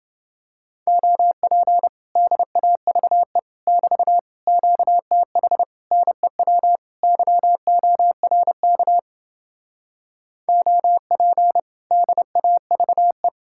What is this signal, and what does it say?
Morse code, heard as a keyed tone, about 31 wpm, 700 Hz. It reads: OP DAVE <BT> QTH NEW YORK OP DAVE